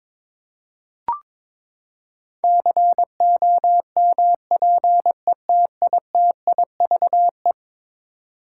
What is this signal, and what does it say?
Morse code, heard as a keyed tone, about 21 wpm, 700 Hz. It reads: COMPETITIVE
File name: practice / 22wpm / Eleven Letter Words - 22wpm